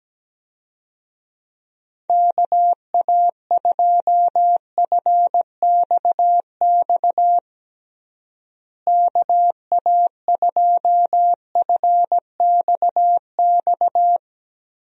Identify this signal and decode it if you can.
Morse code, heard as a keyed tone, about 17 wpm, 700 Hz. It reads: KA2FXX KA2FXX
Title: USA Call Signs - 6 Characters - Repeated 3x - 17wpm